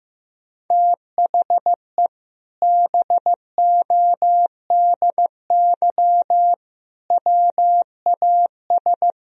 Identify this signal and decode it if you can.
Morse code, heard as a keyed tone, about 15 wpm, 700 Hz. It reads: THE BODY WAS
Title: Sentences from Top 1000 Words - Review Part 2 - 15wpm